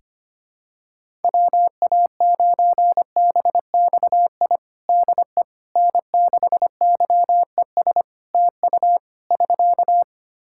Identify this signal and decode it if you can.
Morse code, heard as a keyed tone, about 25 wpm, 700 Hz. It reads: WA9BXI DE N6YES TU <SK>
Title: Practice QSO - Exchange 4 - Ending - 25wpm